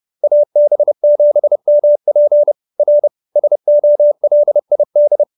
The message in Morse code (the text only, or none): AB7MP R SOLID